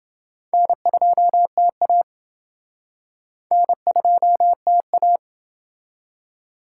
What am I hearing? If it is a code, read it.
Morse code, heard as a keyed tone, about 30 wpm, 700 Hz. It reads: N2TA N2TA